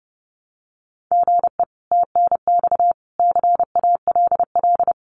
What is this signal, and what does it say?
Morse code, heard as a keyed tone, about 30 wpm, 700 Hz. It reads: GE TNX CALL
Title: Common QSO Phrases 30wpm